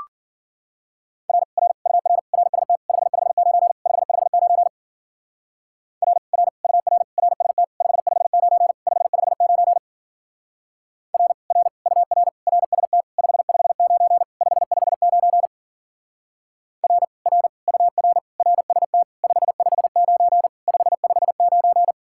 Morse code, heard as a keyed tone, about 49 wpm, 700 Hz. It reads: R R UR RST 559 559 R R UR RST 559 559 R R UR RST 559 559 R R UR RST 559 559
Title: Speed-Racing Common QSO Phrases - 40wpm